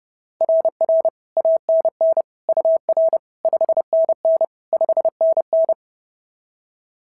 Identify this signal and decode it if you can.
Morse code, heard as a keyed tone, about 31 wpm, 690 Hz. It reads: RR ANN UR 5NN 5NN
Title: SOTA QSO Practice 30wpm